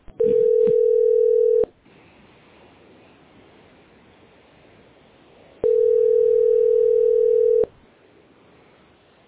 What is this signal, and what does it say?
Ringback tone (North America), 440 + 480 Hz.